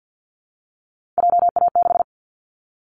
Morse code, heard as a keyed tone, about 50 wpm, 700 Hz. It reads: JA6
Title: Callsign Prefixes for Top 25 DXCC - Repeated 3x - 50wpm